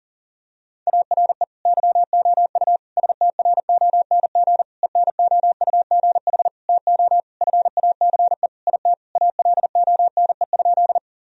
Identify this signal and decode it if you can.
Morse code, heard as a keyed tone, about 40 wpm, 700 Hz. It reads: ARE YOU STRONG ENOUGH TO FACE IT ALONE?
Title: Sentences from Top 1000 Words - Review Part 3 - 40wpm